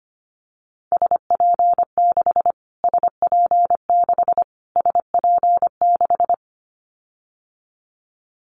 Morse code, heard as a keyed tone, about 25 wpm, 700 Hz. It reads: SP6 SP6 SP6
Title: Callsign Prefixes for Top 25 DXCC - Repeated 3x - 25wpm